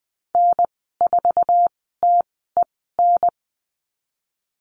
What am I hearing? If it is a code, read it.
Morse code, heard as a keyed tone, about 20 wpm, spaced as for 14 wpm, 700 Hz. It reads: N4TEN
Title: 062 Lesson 11g - Letter h - Call Signs 20wpm